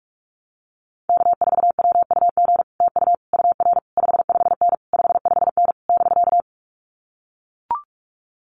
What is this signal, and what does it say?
Morse code, heard as a keyed tone, about 44 wpm, 700 Hz. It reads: K4WUG TU UR 55N 55N <BK>
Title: Common QSO Phrases Encore 45wpm